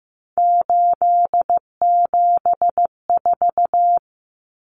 Morse code, heard as a keyed tone, about 15 wpm, 700 Hz. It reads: 874
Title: Numbers - 3-Digit 15wpm